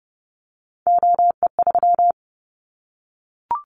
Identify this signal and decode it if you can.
Morse code, heard as a keyed tone, about 30 wpm, 700 Hz. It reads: OE3
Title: Callsign Prefixes for Top 100 DXCC - 30wpm